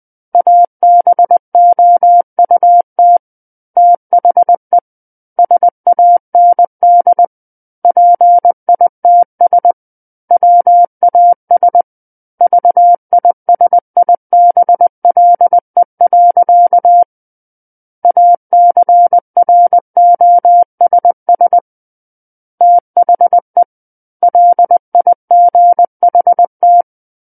Morse code, heard as a keyed tone, about 20 wpm, 700 Hz. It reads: ABOUT THE SAND PITS WAS VISIBLE. ACROSS THE LIGHT
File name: 20 WPM